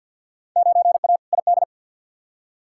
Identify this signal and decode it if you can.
Morse code, heard as a keyed tone, about 46 wpm, 700 Hz. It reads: 9A ID
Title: Field Day Exchanges Encore - 50wpm